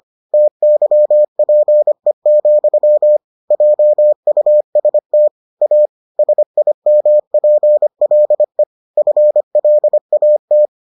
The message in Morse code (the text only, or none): TYPE, JUST A SIMPLE FLAT